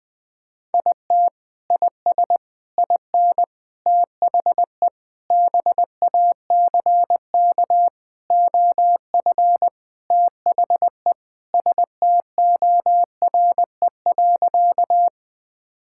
Morse code, heard as a keyed tone, about 20 wpm, 700 Hz. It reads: IT IS IN THE BACK OF THE STORE.
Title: Sentences from Top 1000 Words - Review Part 5 - 20wpm